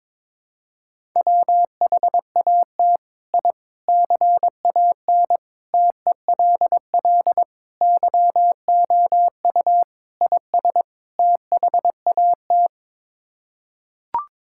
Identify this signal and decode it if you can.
Morse code, heard as a keyed tone, about 22 wpm, 700 Hz. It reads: WHAT I CAN TELL YOU IS THAT
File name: practice / 22wpm / Sets of 7 Words - 22wpm